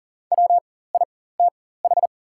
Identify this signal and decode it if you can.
Morse code, heard as a keyed tone, about 40 wpm, spaced as for 19 wpm, 700 Hz. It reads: WITH
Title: 4x Character Spacing - Top 300 Words - 40wpm